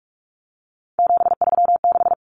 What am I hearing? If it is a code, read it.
Morse code, heard as a keyed tone, about 44 wpm, 700 Hz. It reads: 736
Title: Common Cut Numbers - 3-Digit 45wpm